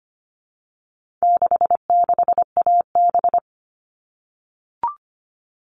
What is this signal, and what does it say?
Morse code, heard as a keyed tone, about 25 wpm, 700 Hz. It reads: 66AB
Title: All Cut Numbers - 4-Digit 25wpm